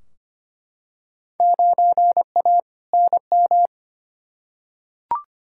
Morse code, heard as a keyed tone, about 25 wpm, 700 Hz. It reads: 9A NM
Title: Field Day Exchanges Encore - 25wpm